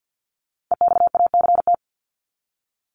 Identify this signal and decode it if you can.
Morse code, heard as a keyed tone, about 48 wpm, 700 Hz. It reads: EXACT
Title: LY Suffix - Top 100 Words with Base Words 50wpm